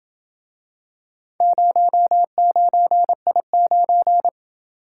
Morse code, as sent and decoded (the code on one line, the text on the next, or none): ----- ----. .. ----.
09I9